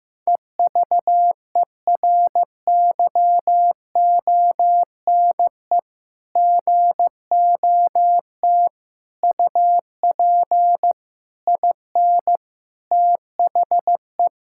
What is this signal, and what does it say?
Morse code, heard as a keyed tone, about 15 wpm, 700 Hz. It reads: EVERYONE GOT UP IN THE
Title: Sentences from Top 1000 Words - Review Part 5 - 15wpm